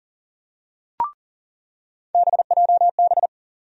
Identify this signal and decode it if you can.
Morse code, heard as a keyed tone, about 40 wpm, 700 Hz. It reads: BJB
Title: USA Call Sign Postfixes - 3 Letters 40wpm